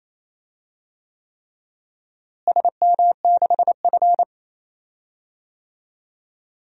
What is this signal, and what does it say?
Morse code, heard as a keyed tone, about 28 wpm, 700 Hz. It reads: SM6F
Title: DXCC-100 Call Signs - 4 Characters - Repeated 3x - 28wpm